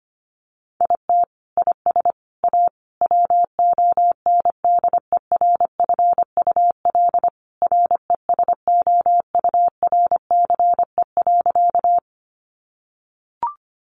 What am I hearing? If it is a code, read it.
Morse code, heard as a keyed tone, about 25 wpm, 700 Hz. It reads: IT IS A WONDERFUL RESOURCE.
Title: Sentences from Top 1300 Words 25wpm